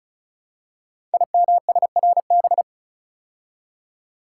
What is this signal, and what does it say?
Morse code, heard as a keyed tone, about 36 wpm, 700 Hz. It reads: IMSRB
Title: Code Groups - 5 Letters - 35wpm